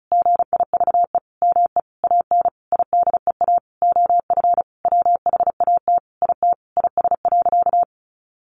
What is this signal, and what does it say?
Morse code, heard as a keyed tone, about 34 wpm, 700 Hz. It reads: GIVE ME AN IDEA OF WHAT IT IS.
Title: Sentences from Top 700 Words 35wpm